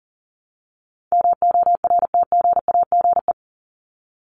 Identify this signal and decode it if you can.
Morse code, heard as a keyed tone, about 40 wpm, 700 Hz. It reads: MORTGAGE
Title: Top 1751-2000 Words 40wpm